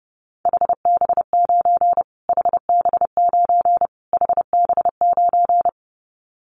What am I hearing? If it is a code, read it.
Morse code, heard as a keyed tone, about 30 wpm, 690 Hz. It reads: HB9 HB9 HB9
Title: Callsign Prefixes for Top 25 DXCC - Repeated 3x - 30wpm